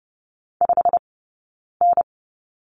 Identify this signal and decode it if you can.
Morse code, heard as a keyed tone, about 30 wpm, spaced as for 9 wpm, 690 Hz. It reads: 5N
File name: course / 30:10wpm / 086 Lesson 15d - Number 5 - Two Characters 30wpm